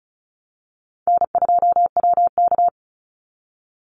Morse code, heard as a keyed tone, about 34 wpm, 700 Hz. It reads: N2WK
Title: Call Signs Repeated 3x - 35wpm